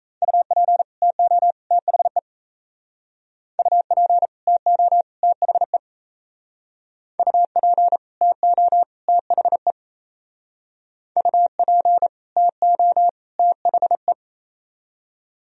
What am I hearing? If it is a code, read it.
Morse code, heard as a keyed tone, about 37 wpm, 700 Hz. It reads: UP TO THE UP TO THE UP TO THE UP TO THE